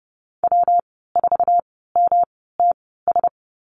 Morse code, heard as a keyed tone, about 30 wpm, spaced as for 17 wpm, 700 Hz. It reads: W4MTS
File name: Lesson 19g - Letter w - Call Signs 30wpm